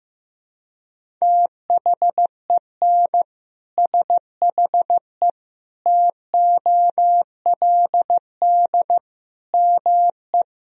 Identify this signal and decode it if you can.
Morse code, heard as a keyed tone, about 15 wpm, 700 Hz. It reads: THEN SHE TOLD ME
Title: Sentences from Top 400 Words and Abbreviations 15wpm